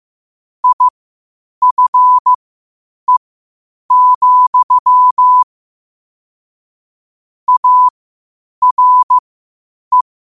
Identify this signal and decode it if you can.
Morse code, heard as a keyed tone, about 15 wpm, spaced as for 8 wpm, 1000 Hz. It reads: IFE, ARE
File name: 8wpm